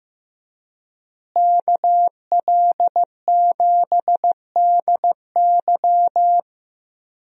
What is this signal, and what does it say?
Morse code, heard as a keyed tone, about 15 wpm, 700 Hz. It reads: KL7DY
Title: DXCC-100 Call Signs - 5 Characters - Repeated 2x - 15wpm